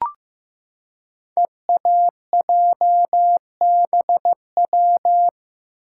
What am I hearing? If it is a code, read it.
Morse code, heard as a keyed tone, about 15 wpm, 700 Hz. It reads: EAJBW
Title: Code Groups - 5 Letters - 15wpm